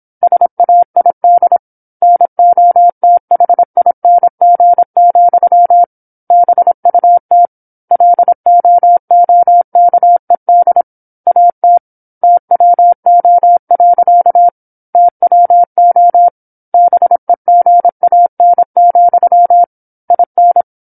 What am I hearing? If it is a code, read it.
Morse code, heard as a keyed tone, about 26 wpm, 700 Hz. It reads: SAID NOTHING, BUT LOOKED AT TWO. TWO BEGAN, IN
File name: 08-Chapter-08 26 WPM